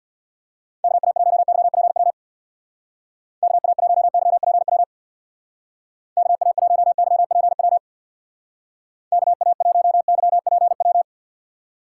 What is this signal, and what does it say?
Morse code, heard as a keyed tone, about 59 wpm, 690 Hz. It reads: KA1YPW KA1YPW KA1YPW KA1YPW
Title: Speed Racing - USA 2x3 Call Signs 50wpm